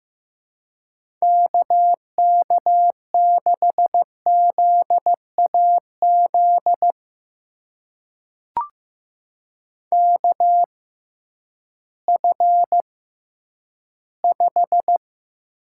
Morse code, heard as a keyed tone, about 15 wpm, spaced as for 10 wpm, 700 Hz. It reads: KK6ZAZ KF5
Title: USA Call Signs - 6 Characters - 6x Character Spacing - 15wpm